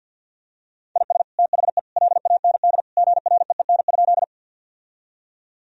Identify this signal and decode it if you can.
Morse code, heard as a keyed tone, about 52 wpm, 700 Hz. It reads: IS THE LAND GREEN?
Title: Sentences from Top 1000 Words - Review Part 2 - 50wpm